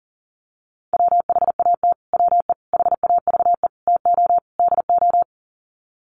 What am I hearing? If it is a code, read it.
Morse code, heard as a keyed tone, about 40 wpm, 700 Hz. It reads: WHAT WE HAVE TO DO
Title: Sets of 5 Words Encore 40wpm